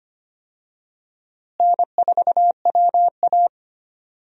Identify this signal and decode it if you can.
Morse code, heard as a keyed tone, about 25 wpm, 700 Hz. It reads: N4WA
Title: USA Call Signs - 4 Characters - Repeated 3x - 25wpm